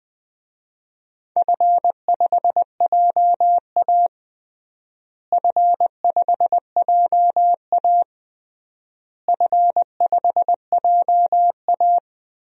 Morse code, heard as a keyed tone, about 21 wpm, 700 Hz. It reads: F5JA F5JA F5JA
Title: DXCC-100 Call Signs - 4 Characters - Repeated 3x - 20wpm